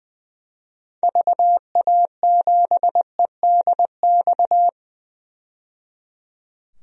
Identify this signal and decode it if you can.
Morse code, heard as a keyed tone, about 20 wpm, 700 Hz. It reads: VA7EDX
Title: DXCC-100 Call Signs - 6 Characters - 20wpm